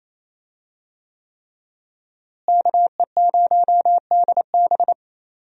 Morse code, heard as a keyed tone, about 28 wpm, 700 Hz. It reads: KE0DB
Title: USA Call Signs - 5 Characters - Repeated 3x - 28wpm